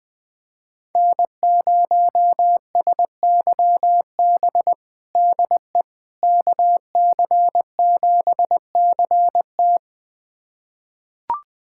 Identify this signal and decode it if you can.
Morse code, heard as a keyed tone, about 20 wpm, 700 Hz. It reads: N0SYB DE KC7CT
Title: USA Callsign DE Callsign - 5 Characters - 20wpm